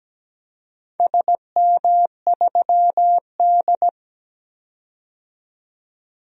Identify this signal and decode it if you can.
Morse code, heard as a keyed tone, about 17 wpm, 700 Hz. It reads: SM3D
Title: DXCC-100 Call Signs - 4 Characters - 17wpm